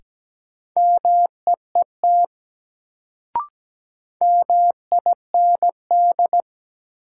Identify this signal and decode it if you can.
Morse code, heard as a keyed tone, about 17 wpm, 700 Hz. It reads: MEET MIND
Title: Sets of 5 Words - New Word Drill - 17wpm